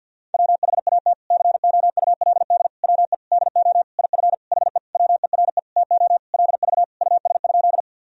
Morse code, heard as a keyed tone, about 47 wpm, 700 Hz. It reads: WHAT COULD WE DO IF HE WERE TO LV US?
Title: Sentences from Top 400 Words and Abbreviations 50wpm